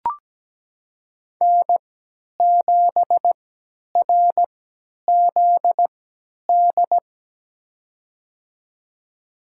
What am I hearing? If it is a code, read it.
Morse code, heard as a keyed tone, about 17 wpm, spaced as for 10 wpm, 700 Hz. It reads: N7RZD
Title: USA Call Signs - 5 Characters - 3x Character Spacing - 17wpm